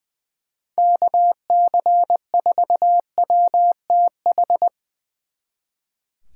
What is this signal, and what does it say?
Morse code, heard as a keyed tone, about 23 wpm, 700 Hz. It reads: KC4WTH
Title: Speed Racing - USA 2x3 Call Signs 20wpm